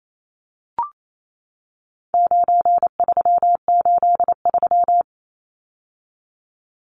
Morse code, heard as a keyed tone, about 28 wpm, 700 Hz. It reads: 9383
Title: Numbers - 4-Digit 28wpm